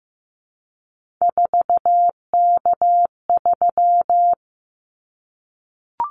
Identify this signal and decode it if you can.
Morse code, heard as a keyed tone, about 15 wpm, 700 Hz. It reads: 4K3
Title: Callsign Prefixes for Top 100 DXCC - 15wpm